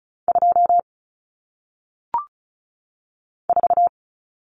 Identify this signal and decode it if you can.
Morse code, heard as a keyed tone, about 36 wpm, 700 Hz. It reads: 2 4
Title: Numbers - 1-Digit 35wpm